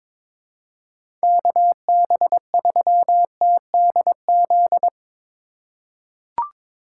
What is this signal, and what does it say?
Morse code, heard as a keyed tone, about 22 wpm, 700 Hz. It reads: KB3TDZ
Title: USA Call Signs - 6 Characters - 2x Character Spacing - 22wpm